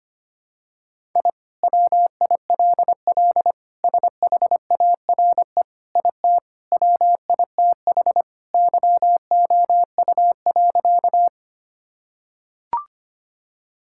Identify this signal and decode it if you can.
Morse code, heard as a keyed tone, about 25 wpm, 700 Hz. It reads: I WILL SHARE IT WITH YOU.
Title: Sentences from Top 1000 Words - Review Part 6 - 25wpm